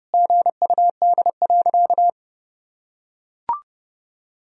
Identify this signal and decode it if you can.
Morse code, heard as a keyed tone, about 32 wpm, 700 Hz. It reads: GUD.
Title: Sentences from Top 200 Words and Abbreviations 30wpm